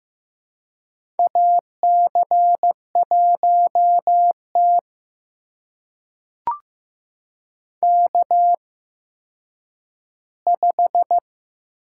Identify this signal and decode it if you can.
Morse code, heard as a keyed tone, about 15 wpm, spaced as for 9 wpm, 700 Hz. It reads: AC1T K5
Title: USA Call Signs - 4 Characters - 8x Character Spacing - 15wpm